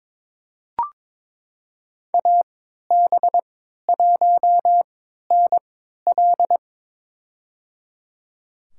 Morse code, heard as a keyed tone, about 22 wpm, spaced as for 13 wpm, 700 Hz. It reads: AB1NL
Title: USA Call Signs - 5 Characters - 3x Character Spacing - 22wpm